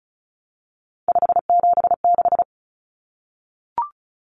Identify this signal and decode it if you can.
Morse code, heard as a keyed tone, about 34 wpm, 700 Hz. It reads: 576
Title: Numbers - 3-Digit 35wpm